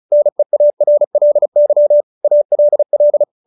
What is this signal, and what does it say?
Morse code, heard as a keyed tone, about 34 wpm, 590 Hz. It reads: NEARLY ALL
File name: QSO314 35